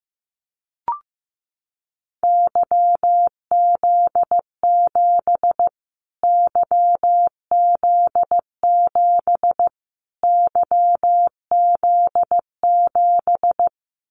Morse code, heard as a keyed tone, about 15 wpm, 700 Hz. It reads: YZ7 YZ7 YZ7
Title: Callsign Prefixes for Top 25 DXCC - Repeated 3x - 15wpm